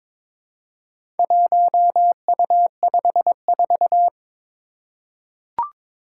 Morse code, heard as a keyed tone, about 23 wpm, 700 Hz. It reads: 1U54
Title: Four Challenging Letter-Number Combinations 22wpm